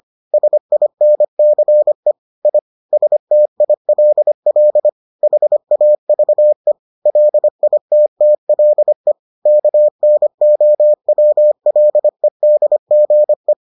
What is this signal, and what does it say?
Morse code, heard as a keyed tone, about 25 wpm, 600 Hz. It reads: SINCE I STILL HAVE LITTLE KNOWLEDGE